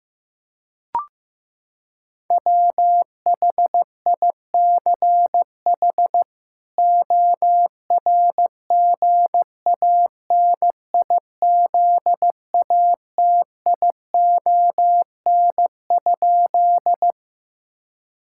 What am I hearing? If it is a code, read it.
Morse code, heard as a keyed tone, about 15 wpm, 700 Hz. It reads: WHICH ORGANIZATION?
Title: Sentences from Top 1000 Words - Review Part 5 - 15wpm